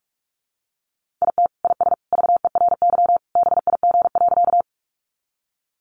Morse code, heard as a keyed tone, about 44 wpm, 700 Hz. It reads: IT IS VERY BIG.
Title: Easy Phrases 45wpm